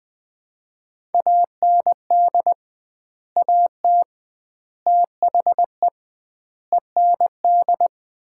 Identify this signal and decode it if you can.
Morse code, heard as a keyed tone, about 21 wpm, 700 Hz. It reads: AND AT THE END
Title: 2x Word-Spacing - Sets of 5 Words - 20wpm